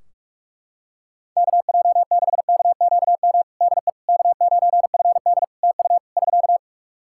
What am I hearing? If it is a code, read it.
Morse code, heard as a keyed tone, about 41 wpm, 700 Hz. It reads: KJ6KQM DE K9FD TU <SK>
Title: K1USN SST QSO Practice 45wpm